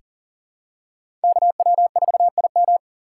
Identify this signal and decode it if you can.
Morse code, heard as a keyed tone, about 40 wpm, 700 Hz. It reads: KW4IM
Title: USA Call Signs - 5 Characters - 6x Character Spacing - 40wpm